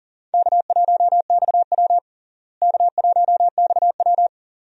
Morse code, heard as a keyed tone, about 40 wpm, 700 Hz. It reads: K1XW K1XW
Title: USA Call Signs - 4 Characters - Repeated 2x - 40wpm